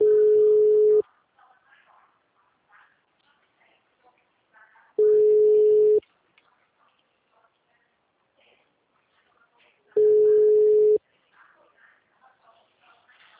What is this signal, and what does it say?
Ringback tone (Europe, 425 Hz), with speech.